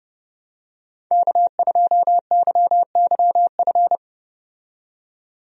Morse code, heard as a keyed tone, about 30 wpm, 700 Hz. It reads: K2YYF
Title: USA Call Signs - 5 Characters - 30wpm